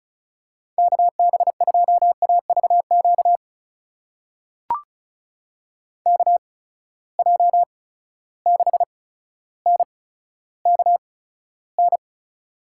Morse code, heard as a keyed tone, about 36 wpm, spaced as for 15 wpm, 700 Hz. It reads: KB2AVQ KJ6NKN
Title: USA Call Signs - 6 Characters - 8x Character Spacing - 35wpm